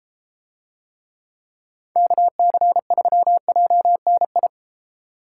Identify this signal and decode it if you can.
Morse code, heard as a keyed tone, about 32 wpm, 700 Hz. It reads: KC3JNI